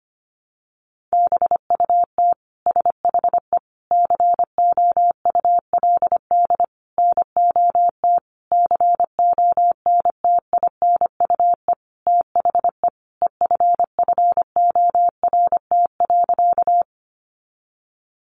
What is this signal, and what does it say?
Morse code, heard as a keyed tone, about 25 wpm, 700 Hz. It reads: BUT SHE COULD NOT CONTINUE THE EFFORT.
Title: Sentences from Top 1200 Words 25wpm